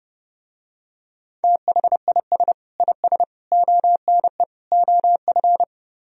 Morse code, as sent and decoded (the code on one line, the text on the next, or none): - .... .. ... / .. ... / --- -. / --- ..-.
THIS IS ONE OF